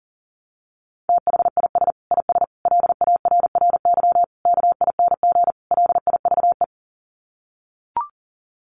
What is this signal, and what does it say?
Morse code, heard as a keyed tone, about 40 wpm, 700 Hz. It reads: THIS IS LARRY KING LIVE
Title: Sets of 5 Words Encore2 40wpm